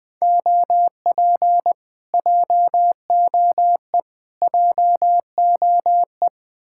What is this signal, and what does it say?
Morse code, heard as a keyed tone, about 21 wpm, 700 Hz. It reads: OP JOE JOE